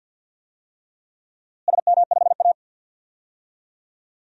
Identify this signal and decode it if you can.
Morse code, heard as a keyed tone, about 51 wpm, 700 Hz. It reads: SM5A
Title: DXCC-100 Call Signs - 4 Characters - Repeated 3x - 50wpm